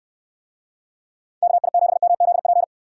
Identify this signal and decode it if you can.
Morse code, heard as a keyed tone, about 69 wpm, 700 Hz. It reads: KI7MZP